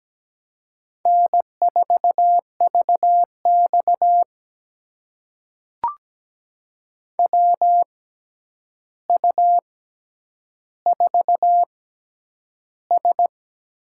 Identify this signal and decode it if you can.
Morse code, heard as a keyed tone, about 17 wpm, spaced as for 9 wpm, 700 Hz. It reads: N4VX WU4S
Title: USA Call Signs - 4 Characters - 6x Character Spacing - 17wpm